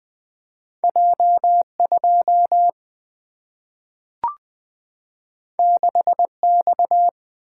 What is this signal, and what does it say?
Morse code, heard as a keyed tone, about 20 wpm, 700 Hz. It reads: J2 6X
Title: Single-Letter Single-Number 20wpm